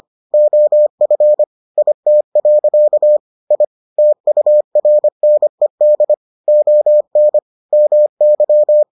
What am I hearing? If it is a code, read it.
Morse code, heard as a keyed tone, about 25 wpm, 600 Hz. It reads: OF IT. I TURNED ON MY